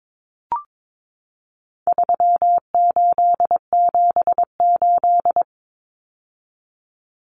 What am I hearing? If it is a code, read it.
Morse code, heard as a keyed tone, about 22 wpm, 700 Hz. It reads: 3878
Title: Numbers - 4-Digit 22wpm